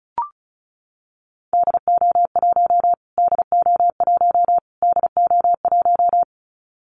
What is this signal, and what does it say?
Morse code, heard as a keyed tone, about 34 wpm, 700 Hz. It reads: DO1 DO1 DO1
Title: Callsign Prefixes for Top 25 DXCC - Repeated 3x - 35wpm